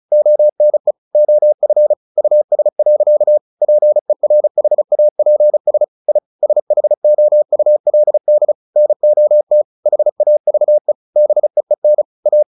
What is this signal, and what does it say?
Morse code, heard as a keyed tone, about 34 wpm, 600 Hz. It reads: ONE OF US. PERHAPS I SHOULD NOT HAVE BEEN A
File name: the old man and the sea / the old man and the sea 1 35wpm